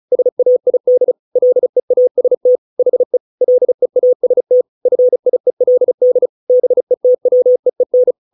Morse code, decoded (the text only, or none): SAID LEAST HE LEAST FIELD BETWEEN